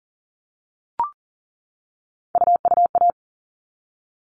Morse code, heard as a keyed tone, about 40 wpm, 700 Hz. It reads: UUA
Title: All Cut Numbers - 3-Digit 40wpm